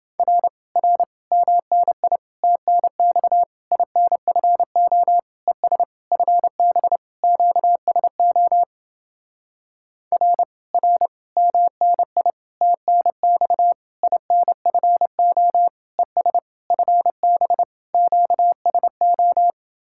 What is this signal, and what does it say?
Morse code, heard as a keyed tone, about 28 wpm, 700 Hz. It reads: R R MNI TNX INFO ES FB QSO R R MNI TNX INFO ES FB QSO